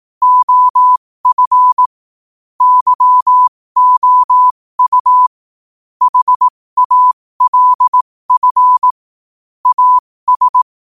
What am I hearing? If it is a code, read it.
Morse code, heard as a keyed tone, about 18 wpm, spaced as for 15 wpm, 1000 Hz. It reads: OF YOU HALF AS
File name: qotdfiles / 15wpm